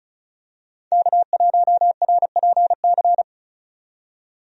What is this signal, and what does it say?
Morse code, heard as a keyed tone, about 34 wpm, 700 Hz. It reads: K1RPC
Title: USA Call Signs - 5 Characters - 35wpm